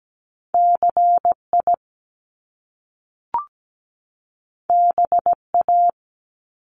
Morse code, heard as a keyed tone, about 17 wpm, 700 Hz. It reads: CI BA
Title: Top Two Letter Combinations in Words 17wpm